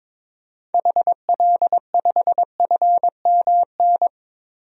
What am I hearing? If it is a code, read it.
Morse code, heard as a keyed tone, about 22 wpm, 700 Hz. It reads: HL5FMN
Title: DXCC-100 Call Signs - 6 Characters - 22wpm